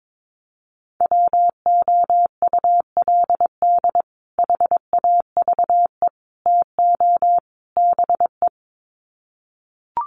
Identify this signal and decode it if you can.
Morse code, heard as a keyed tone, about 22 wpm, 700 Hz. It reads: WOULD HAVE TO BE
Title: Sets of 4 Words 22wpm